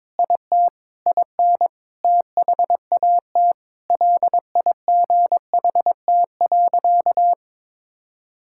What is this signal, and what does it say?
Morse code, heard as a keyed tone, about 22 wpm, 700 Hz. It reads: IT IN THAT LIGHT.
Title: Sentences from Top 1000 Words - Review Part 6 - 22wpm